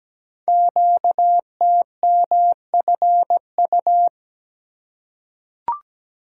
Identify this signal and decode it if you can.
Morse code, heard as a keyed tone, about 17 wpm, 700 Hz. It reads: QTMFU
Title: Code Groups - 5 Letters - 17wpm